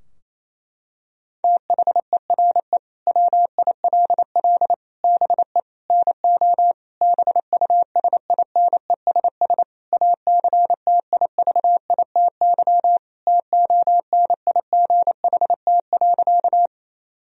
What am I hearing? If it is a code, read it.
Morse code, heard as a keyed tone, about 28 wpm, 700 Hz. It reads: THERE WILL BE NO BUSINESS ACTIVITY TONIGHT.
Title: Sentences from Top 1000 Words - Review Part 4 - 28wpm